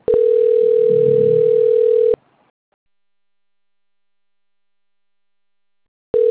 Ringback tone (North America), 440 + 480 Hz.
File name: rg-1600-unknown- - - .83721